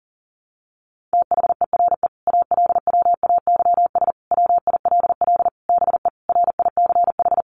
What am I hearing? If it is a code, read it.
Morse code, heard as a keyed tone, about 40 wpm, 700 Hz. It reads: THERE ALWAYS WILL BE RICH
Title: Sentences from Top 1100 Words 40wpm